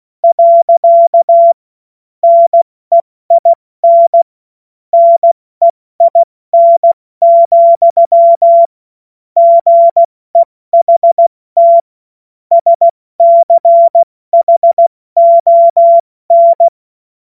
Morse code, heard as a keyed tone, about 16 wpm, 680 Hz. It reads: . NEIN NEIN, GEHT SCHON